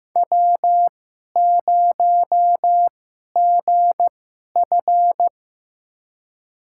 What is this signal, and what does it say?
Morse code, heard as a keyed tone, about 15 wpm, spaced as for 11 wpm, 700 Hz. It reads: W0GF
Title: USA Call Signs - 4 Characters - 2x Character Spacing - 15wpm